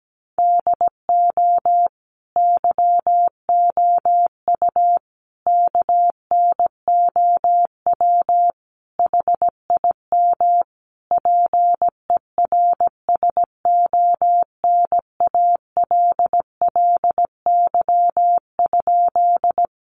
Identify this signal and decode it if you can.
Morse code, heard as a keyed tone, about 17 wpm, 700 Hz. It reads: DO YOU KNOW HIM PERSONALLY?
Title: Sentences from Top 1400 Words 17wpm